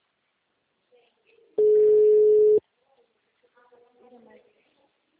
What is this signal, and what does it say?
Ringback tone (Europe, 425 Hz), with speech.
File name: out- -2014- - - .6962